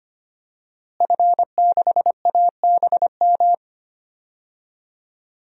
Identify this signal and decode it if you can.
Morse code, heard as a keyed tone, about 25 wpm, 700 Hz. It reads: F6ABM
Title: DXCC-100 Call Signs - 5 Characters - Repeated 3x - 25wpm